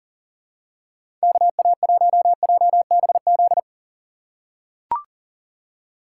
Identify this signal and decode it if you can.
Morse code, heard as a keyed tone, about 40 wpm, 700 Hz. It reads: KA1JBZ
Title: USA Call Signs - 4x Character Spacing - 40wpm